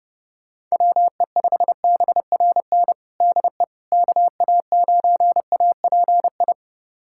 Morse code, heard as a keyed tone, about 30 wpm, 700 Hz. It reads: WE5BRN DE KA9API
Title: USA Callsign DE Callsign - 6 Characters - 30wpm